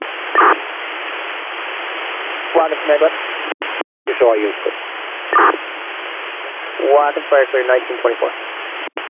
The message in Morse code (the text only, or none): none